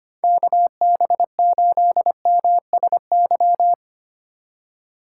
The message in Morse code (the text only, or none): KB8MSY